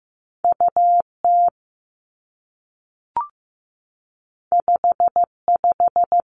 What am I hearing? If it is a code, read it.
Morse code, heard as a keyed tone, about 15 wpm, 700 Hz. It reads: U T 5 5
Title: Two Challenging Letter-Number Combinations 15wpm